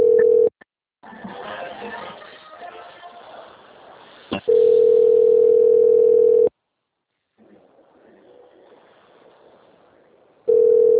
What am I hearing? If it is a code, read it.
Ringback tone (North America), 440 + 480 Hz, with speech.
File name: out- - - - - .29954